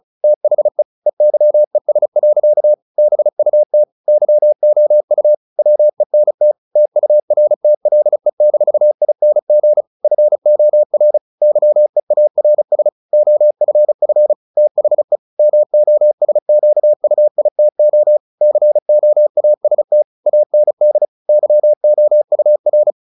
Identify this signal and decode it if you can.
Morse code, heard as a keyed tone, about 35 wpm, 600 Hz. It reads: THE EYES. BUT YOU WENT TURTLE-ING FOR YEARS OFF THE MOSQUITO COAST AND YOUR
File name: the old man and the sea / the old man and the sea 1 35wpm